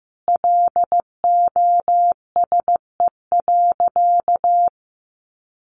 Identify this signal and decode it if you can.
Morse code, heard as a keyed tone, about 15 wpm, 700 Hz. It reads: LOSE.
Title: Sentences from Top 1100 Words 15wpm